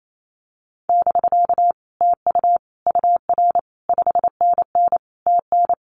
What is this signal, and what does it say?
Morse code, heard as a keyed tone, about 28 wpm, 700 Hz. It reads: <BK> TU UR 5NN TN
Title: Common QSO Phrases Encore 28wpm